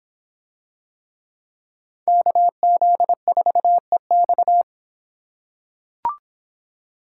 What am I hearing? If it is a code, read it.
Morse code, heard as a keyed tone, about 26 wpm, 700 Hz. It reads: KZ4EX